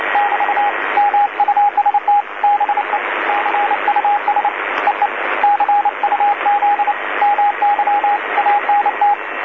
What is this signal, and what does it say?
Morse code, heard as a keyed tone, about 28 wpm, 820 Hz, with radio noise. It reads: <BT> MUST BE CUS EE CUZ MY ANT
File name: DC-TRCVR / 80-CW